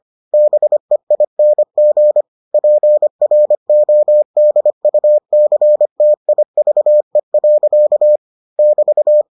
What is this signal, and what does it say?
Morse code, heard as a keyed tone, about 25 wpm, 600 Hz. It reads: BEING PRODUCTIVE. <BT>